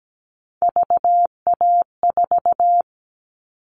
Morse code, heard as a keyed tone, about 17 wpm, 700 Hz. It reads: VA4
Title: All Cut Numbers - 3-Digit 17wpm